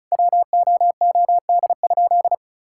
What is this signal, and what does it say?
Morse code, heard as a keyed tone, about 36 wpm, 700 Hz. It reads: WOOD?
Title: Sentences from Top 1600 Words 35wpm